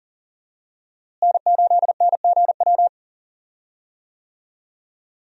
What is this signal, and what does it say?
Morse code, heard as a keyed tone, about 40 wpm, 700 Hz. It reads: N8NGW
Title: USA Call Signs - 5 Characters - Repeated 2x - 40wpm